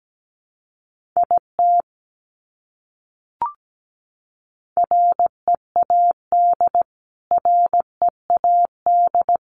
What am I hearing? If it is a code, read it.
Morse code, heard as a keyed tone, about 17 wpm, 700 Hz. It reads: IT READ READ
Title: Top 200 Words Repeated 17wpm